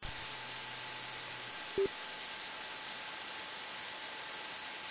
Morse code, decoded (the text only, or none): none